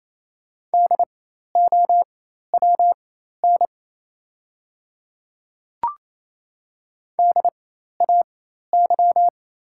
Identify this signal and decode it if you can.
Morse code, heard as a keyed tone, about 27 wpm, spaced as for 13 wpm, 690 Hz. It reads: DOWN DAY
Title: 4x Character Spacing - Top 100 Words - 28wpm